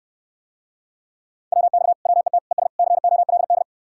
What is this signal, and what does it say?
Morse code, heard as a keyed tone, about 69 wpm, 700 Hz. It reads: WX WET ES COLD